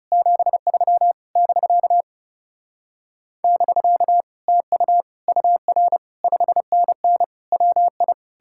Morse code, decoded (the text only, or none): 73 <BK> <BK> TU UR 5NN WI